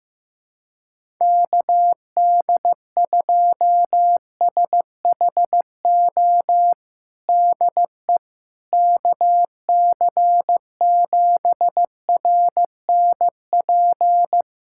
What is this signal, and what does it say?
Morse code, heard as a keyed tone, about 15 wpm, 700 Hz. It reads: KD2SHO DE KC7RNP
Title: USA Callsign DE Callsign - 6 Characters - 15wpm